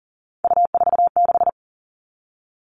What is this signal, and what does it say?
Morse code, heard as a keyed tone, about 38 wpm, 700 Hz. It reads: U46
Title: Three Challenging Letter-Number Combinations 40wpm